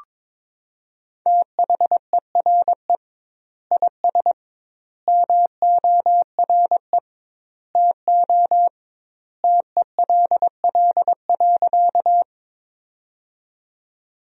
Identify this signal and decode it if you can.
Morse code, heard as a keyed tone, about 22 wpm, 700 Hz. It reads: THERE IS MORE TO TELL.
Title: 2x Word-Spacing - Sentences from Top 200 Words - 22wpm